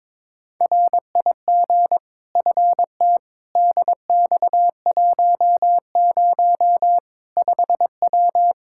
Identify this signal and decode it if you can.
Morse code, heard as a keyed tone, about 22 wpm, 700 Hz. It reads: RIG FT DX10 5W
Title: Practice QSO - Exchange 2 - Info - 22wpm